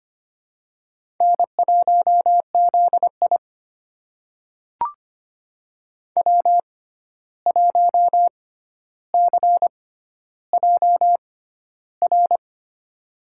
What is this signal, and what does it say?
Morse code, heard as a keyed tone, about 25 wpm, spaced as for 12 wpm, 700 Hz. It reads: N1ZI W1CJR
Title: USA Call Signs - 6x Character Spacing - 25wpm